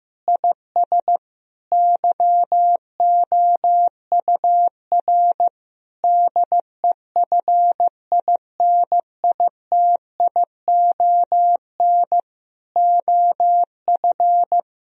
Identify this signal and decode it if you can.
Morse code, heard as a keyed tone, about 15 wpm, 700 Hz. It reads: IS YOUR DEFINITION OF
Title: Sentences from Top 1000 Words - Review Part 4 - 15wpm